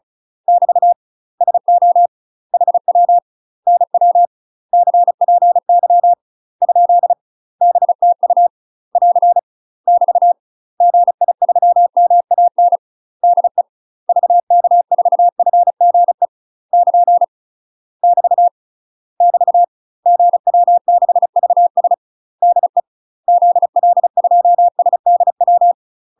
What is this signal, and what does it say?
Morse code, heard as a keyed tone, about 34 wpm, 700 Hz. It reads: <BT> SO HW NW CPY ? BTU <AR> <BT> GI3MAN DE VK4FGE <KN> <BT> <BT> GW6VS DE ZL2SDW
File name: UK MASTER QSOs TRAINING / 35WPM / UK MASTER QSOs 0001